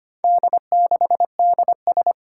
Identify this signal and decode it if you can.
Morse code, heard as a keyed tone, about 25 wpm, 700 Hz. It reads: D6DS
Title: Four Challenging Letter-Number Combinations 25wpm